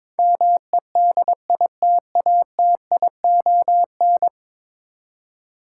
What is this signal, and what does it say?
Morse code, heard as a keyed tone, about 21 wpm, 700 Hz. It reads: MEDITATION
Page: morse-code-ninja-practice-22wpm